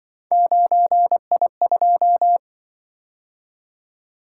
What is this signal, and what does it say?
Morse code, heard as a keyed tone, about 25 wpm, 700 Hz. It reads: 9I2
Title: Speed-Racing - Three Letter-Number Combinations - 20wpm